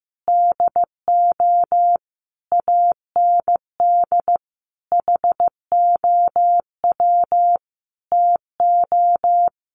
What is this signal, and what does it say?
Morse code, heard as a keyed tone, about 15 wpm, 700 Hz. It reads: DO AND HOW TO
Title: Sentences from Top 500 Words 15wpm